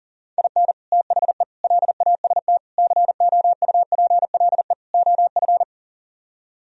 Morse code, heard as a keyed tone, about 50 wpm, 700 Hz. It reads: IN THE LAST COUPLE OF